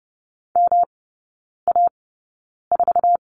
Morse code, heard as a keyed tone, about 30 wpm, spaced as for 9 wpm, 700 Hz. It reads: MA4
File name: Lesson 18e - Letter m - Three Characters 30wpm